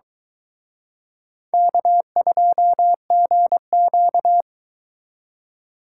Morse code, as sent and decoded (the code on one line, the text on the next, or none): -.- ..--- --. --.-
K2GQ